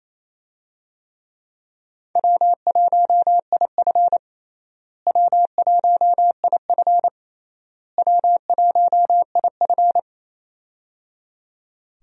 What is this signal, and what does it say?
Morse code, heard as a keyed tone, about 28 wpm, 700 Hz. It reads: W1IF W1IF W1IF
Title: USA Call Signs - 4 Characters - Repeated 3x - 28wpm